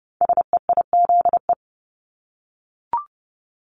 Morse code, heard as a keyed tone, about 30 wpm, 700 Hz. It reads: SEIZE
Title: Multi-letter Phonemes Repeated 3x 30wpm